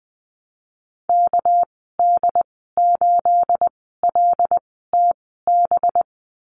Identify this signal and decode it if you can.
Morse code, heard as a keyed tone, about 20 wpm, spaced as for 15 wpm, 700 Hz. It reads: KD8LTB
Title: 219 Lesson 35g - Number 0 - Call Signs 20wpm